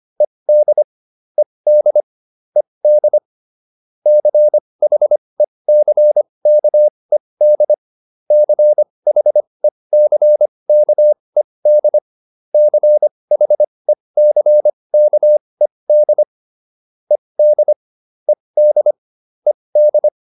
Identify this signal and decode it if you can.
Morse code, heard as a keyed tone, about 25 wpm, spaced as for 20 wpm, 600 Hz. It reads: ED ED ED CHECKED CHECKED CHECKED ED ED ED